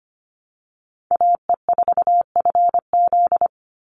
Morse code, heard as a keyed tone, about 25 wpm, 700 Hz. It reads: AE4FZ
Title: NA SOTA Chaser Callsigns 25wpm